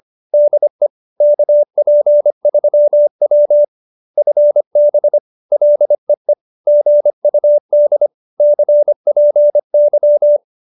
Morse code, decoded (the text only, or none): DE KP3W FB LEE GUD CPY